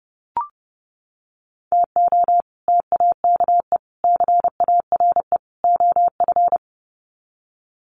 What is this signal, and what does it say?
Morse code, heard as a keyed tone, about 30 wpm, 700 Hz. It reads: TO TAKE CARE OF
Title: Sets of 4 Words 30wpm